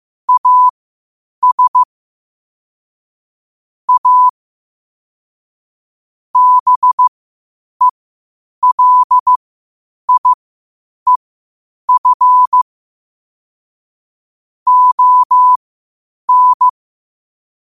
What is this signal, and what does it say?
Morse code, heard as a keyed tone, about 15 wpm, spaced as for 8 wpm, 1000 Hz. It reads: AS A BELIEF ON
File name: qotdfiles / 8wpm